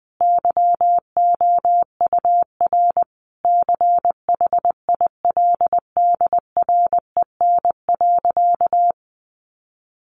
Morse code, heard as a keyed tone, about 20 wpm, 700 Hz. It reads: YOUR CHILDREN.
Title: Sentences from Top 1200 Words 20wpm